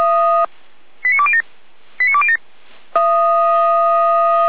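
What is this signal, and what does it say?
Five-tone selective call (ZVEI), heard as quick runs of five tones, tones of about 70 ms. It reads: 89207 89207
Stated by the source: ZVEI five-tone selective call 89191, received by SDR.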